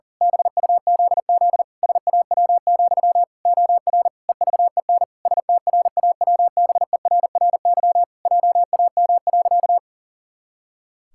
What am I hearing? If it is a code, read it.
Morse code, heard as a keyed tone, about 40 wpm, 700 Hz. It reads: BUZZ SAW, OR EVEN STRAWBERRY JAM.
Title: Pangram Challenge v2 - 40wpm